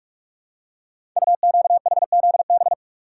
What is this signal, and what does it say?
Morse code, heard as a keyed tone, about 44 wpm, 690 Hz. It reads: UQHZB
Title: Code Groups - 5 Letters - 45wpm